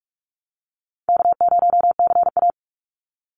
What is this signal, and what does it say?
Morse code, heard as a keyed tone, about 44 wpm, 700 Hz. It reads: K0CA
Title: Easy Call Signs Repeated 2x - 45wpm